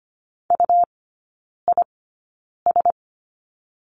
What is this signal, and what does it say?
Morse code, heard as a keyed tone, about 25 wpm, spaced as for 9 wpm, 700 Hz. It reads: UIS
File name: Lesson 17e - Letter u - Three Characters 25wpm